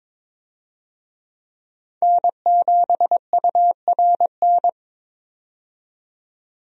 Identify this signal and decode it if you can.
Morse code, heard as a keyed tone, about 22 wpm, 700 Hz. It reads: N7URN
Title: USA Call Signs - 5 Characters - 22wpm